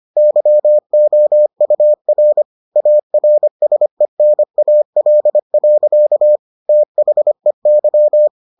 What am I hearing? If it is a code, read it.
Morse code, heard as a keyed tone, about 25 wpm, 600 Hz. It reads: YOUR ARSENAL. THEY